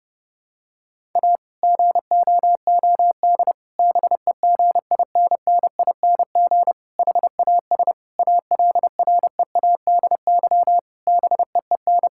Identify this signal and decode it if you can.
Morse code, heard as a keyed tone, about 30 wpm, 700 Hz. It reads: A GOOD BEGINNING HAS ALREADY BEEN
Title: Sentences from Top 1000 Words - Review Part 5 - 30wpm